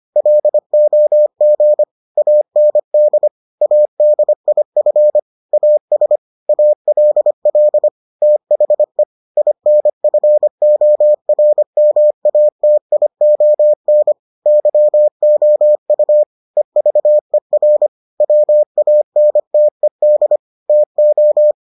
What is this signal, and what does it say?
Morse code, heard as a keyed tone, about 25 wpm, 600 Hz. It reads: LOG AND ADIF AS ALL THE INFORMATION YOU EVER WANTED TO